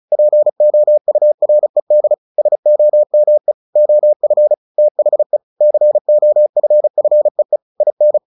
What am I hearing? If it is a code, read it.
Morse code, heard as a keyed tone, about 36 wpm, 600 Hz. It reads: POURED SOME OF THE COFFEE IN
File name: the old man and the sea 2 35wpm